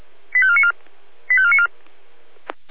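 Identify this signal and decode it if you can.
Five-tone selective call (ZVEI), heard as quick runs of five tones, tones of about 70 ms. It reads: 85494 85494
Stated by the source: ZVEI five-tone selective call received by SDR.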